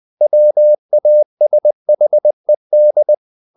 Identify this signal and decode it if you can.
Morse code, heard as a keyed tone, about 20 wpm, 600 Hz. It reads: WASHED
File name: the old man and the sea part 2 20wpm